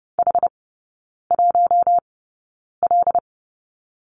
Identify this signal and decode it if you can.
Morse code, heard as a keyed tone, about 30 wpm, spaced as for 9 wpm, 700 Hz. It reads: H1L
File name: Lesson 15e - Number 5 - Three Characters 30wpm